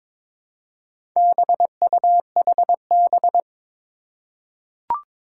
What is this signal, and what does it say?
Morse code, heard as a keyed tone, about 22 wpm, 700 Hz. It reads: BUHB
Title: Four Challenging Letter-Number Combinations 22wpm